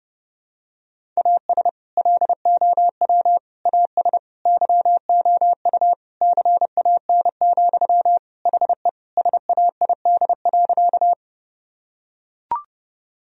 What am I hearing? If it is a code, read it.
Morse code, heard as a keyed tone, about 30 wpm, 700 Hz. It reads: AS LOW AS YOU CAN, HE SAID.
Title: Sentences from Top 1000 Words - Review Part 1 - 30wpm